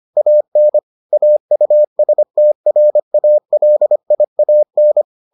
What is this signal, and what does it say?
Morse code, heard as a keyed tone, about 25 wpm, 600 Hz. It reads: AN AUSTRALIAN